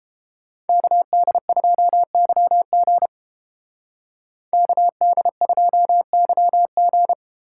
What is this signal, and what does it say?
Morse code, heard as a keyed tone, about 30 wpm, 700 Hz. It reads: KD2YG KD2YG